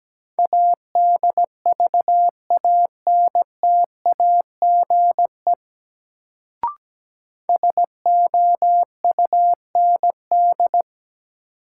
Morse code, heard as a keyed tone, about 17 wpm, 700 Hz. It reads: ADVANTAGE SOUND
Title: Sentences from Top 900 Words - New Word Drill - 17wpm